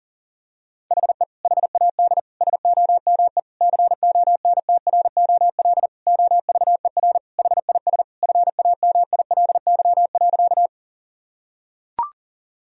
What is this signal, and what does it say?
Morse code, heard as a keyed tone, about 40 wpm, 700 Hz. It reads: HE HAD SOME CONTROL OVER HIS FAMILY.
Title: Sentences from Top 1000 Words - Review Part 3 - 40wpm